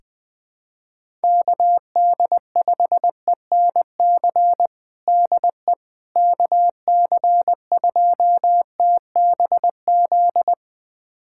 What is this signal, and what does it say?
Morse code, heard as a keyed tone, about 20 wpm, 700 Hz. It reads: KD5ENC DE KC2TBZ
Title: USA Callsign DE Callsign - 6 Characters - 20wpm